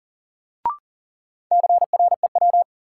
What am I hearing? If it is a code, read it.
Morse code, heard as a keyed tone, about 38 wpm, 700 Hz. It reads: CREW
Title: Sentences from Top 1500 Words - New Word Drill Alt - 40wpm